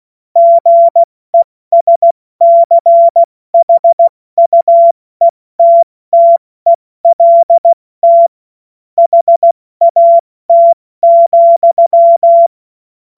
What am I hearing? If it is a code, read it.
Morse code, heard as a keyed tone, about 16 wpm, 680 Hz. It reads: GESCHUETTELT HAT,
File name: Morsekurs / Witze / Witz 14.November 2023 70 F80